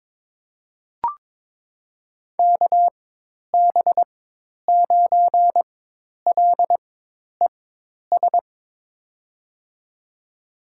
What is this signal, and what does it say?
Morse code, heard as a keyed tone, about 22 wpm, spaced as for 10 wpm, 700 Hz. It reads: KB9LES
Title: USA Call Signs - 6 Characters - 4x Character Spacing - 22wpm